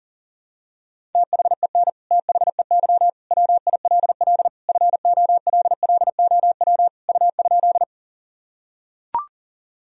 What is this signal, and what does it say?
Morse code, heard as a keyed tone, about 40 wpm, 700 Hz. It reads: THEN THEY WILL FOLLOW U?
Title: Sentences from Top 200 Words and Abbreviations 40wpm